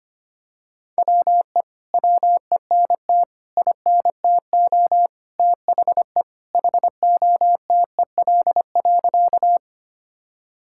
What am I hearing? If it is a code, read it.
Morse code, heard as a keyed tone, about 25 wpm, 700 Hz. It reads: WE WENT INTO THE HOTEL.
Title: Sentences from Top 1600 Words 25wpm